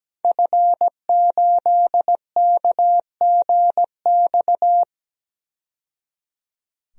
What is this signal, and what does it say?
Morse code, heard as a keyed tone, about 17 wpm, 700 Hz. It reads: F8KGX
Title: DXCC-100 Call Signs - 5 Characters - 17wpm